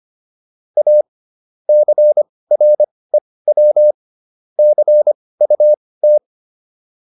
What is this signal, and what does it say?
Morse code, heard as a keyed tone, about 25 wpm, spaced as for 18 wpm, 600 Hz. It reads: A CREW CUT